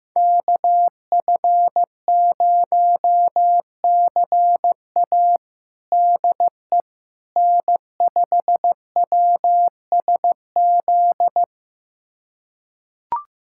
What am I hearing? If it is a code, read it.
Morse code, heard as a keyed tone, about 15 wpm, 700 Hz. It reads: KF0CA DE N5WSZ
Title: USA Callsign DE Callsign - 5 Characters - 15wpm